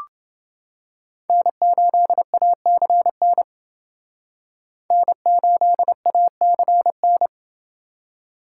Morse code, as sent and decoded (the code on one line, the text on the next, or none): -. ---.. .- -.-. -. / -. ---.. .- -.-. -.
N8ACN N8ACN